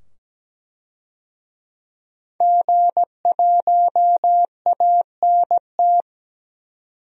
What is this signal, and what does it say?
Morse code, heard as a keyed tone, about 17 wpm, 700 Hz. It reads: G1ANT
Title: DXCC-100 Call Signs - 5 Characters - 17wpm